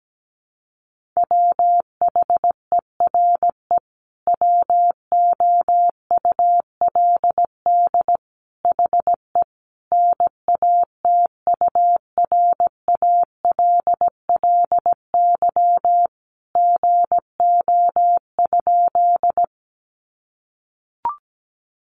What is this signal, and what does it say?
Morse code, heard as a keyed tone, about 17 wpm, 700 Hz. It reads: WHERE WOULD HE NATURALLY GO?
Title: Sentences from Top 1300 Words 17wpm